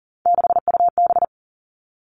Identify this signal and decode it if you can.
Morse code, heard as a keyed tone, about 40 wpm, 700 Hz. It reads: 6UB
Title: All Cut Numbers - 3-Digit 40wpm